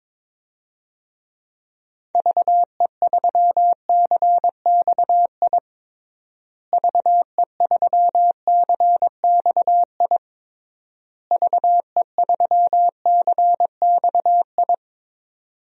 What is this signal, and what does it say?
Morse code, heard as a keyed tone, about 22 wpm, 700 Hz. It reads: VE3CXI VE3CXI VE3CXI
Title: DXCC-100 Call Signs - 6 Characters - Repeated 3x - 22wpm